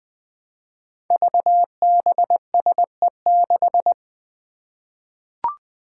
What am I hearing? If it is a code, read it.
Morse code, heard as a keyed tone, about 20 wpm, 700 Hz. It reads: VBSE6
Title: Five Challenging Letter-Number Combinations 20wpm